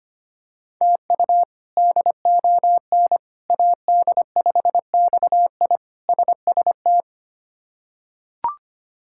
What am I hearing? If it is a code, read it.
Morse code, heard as a keyed tone, about 25 wpm, 700 Hz. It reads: TU DON AD5XI SST
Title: K1USN SST QSO Practice - 25wpm